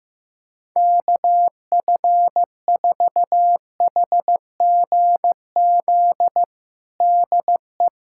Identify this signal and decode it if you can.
Morse code, heard as a keyed tone, about 15 wpm, 700 Hz. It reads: KF4HGZ DE
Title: USA Callsign DE Callsign - 6 Characters - 15wpm